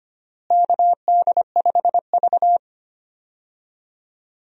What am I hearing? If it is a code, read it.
Morse code, heard as a keyed tone, about 25 wpm, 700 Hz. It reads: KD5V
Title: USA Call Signs - 4 Characters - Repeated 3x - 25wpm